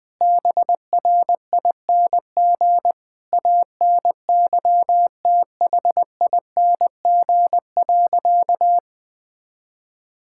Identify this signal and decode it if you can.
Morse code, heard as a keyed tone, about 20 wpm, 700 Hz. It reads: BRING ANYTHING.
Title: Sentences with 4-Word Phrases - 20wpm